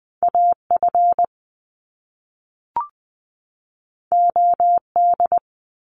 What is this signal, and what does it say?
Morse code, heard as a keyed tone, about 20 wpm, 700 Hz. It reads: A F O D
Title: Top Two Letter Combinations in Words 20wpm